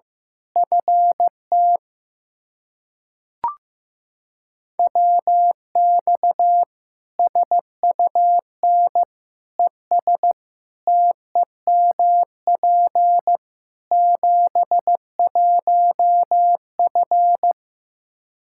Morse code, heard as a keyed tone, about 15 wpm, 700 Hz. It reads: FT WX SUN ES TEMP 71F
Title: Practice QSO - Exchange 2 - Info - 15wpm